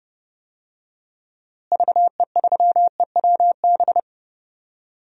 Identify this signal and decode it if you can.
Morse code, heard as a keyed tone, about 30 wpm, 700 Hz. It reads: VE3EWB
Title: DXCC-100 Call Signs - 6 Characters - Repeated 2x - 30wpm